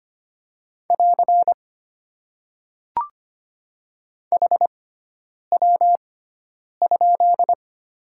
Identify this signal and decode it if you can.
Morse code, heard as a keyed tone, about 25 wpm, spaced as for 9 wpm, 700 Hz. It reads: <AR> HW?
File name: Top 90 QSO Elements - 6x Character Spacing - 25wpm